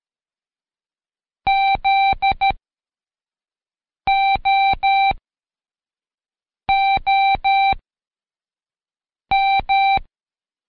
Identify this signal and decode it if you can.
Morse code, heard as a keyed tone, about 13 wpm, spaced as for 5 wpm, 780 Hz. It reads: ZOOM
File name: Code Quick / AudioDownld2 / Seg102-Day12-FinalDrill